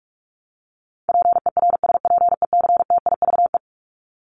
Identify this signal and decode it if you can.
Morse code, heard as a keyed tone, about 44 wpm, 700 Hz. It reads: PERSPECTIVE
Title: Top 1001-1250 Words 45wpm